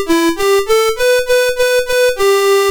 There is no code.